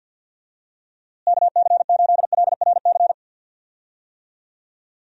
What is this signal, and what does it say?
Morse code, heard as a keyed tone, about 51 wpm, 700 Hz. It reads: KC8LRC
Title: USA Call Signs - 6 Characters - Repeated 3x - 50wpm